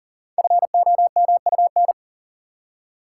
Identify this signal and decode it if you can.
Morse code, heard as a keyed tone, about 39 wpm, 700 Hz. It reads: FOMUN
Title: Code Groups - 5 Letters - 40wpm